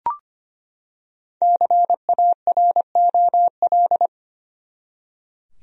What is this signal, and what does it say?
Morse code, heard as a keyed tone, about 25 wpm, 700 Hz. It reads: CAROL
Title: USA Common Names - Max 5 Letters - 25wpm